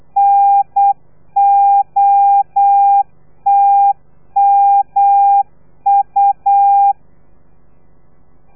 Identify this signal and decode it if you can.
Morse code, heard as a keyed tone, about 8 wpm, 800 Hz. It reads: NOTMU
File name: telegrafia / i2tfj / Lez3 35